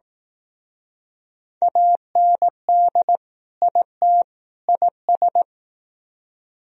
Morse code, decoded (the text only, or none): AND IT IS